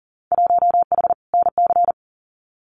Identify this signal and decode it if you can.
Morse code, heard as a keyed tone, about 40 wpm, 700 Hz. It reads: 1H NC
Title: Winter Field Day Exchanges 40wpm